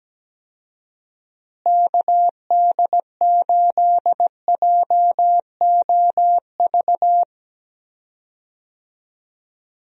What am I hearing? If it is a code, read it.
Morse code, heard as a keyed tone, about 17 wpm, 700 Hz. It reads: KD8JOV
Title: USA Call Signs - 6 Characters - Repeated 3x - 17wpm